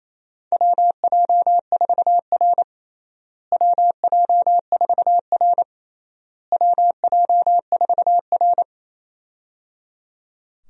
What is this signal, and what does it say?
Morse code, heard as a keyed tone, about 28 wpm, 700 Hz. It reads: WJ4R WJ4R WJ4R
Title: USA Call Signs - 4 Characters - Repeated 3x - 28wpm